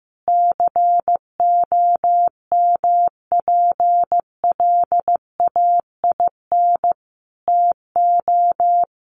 Morse code, heard as a keyed tone, about 15 wpm, 700 Hz. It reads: COMPLAIN TO
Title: Sentences from Top 1300 Words 15wpm